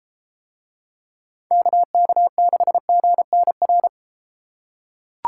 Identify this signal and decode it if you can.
Morse code, heard as a keyed tone, about 31 wpm, 700 Hz. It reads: KK6GNR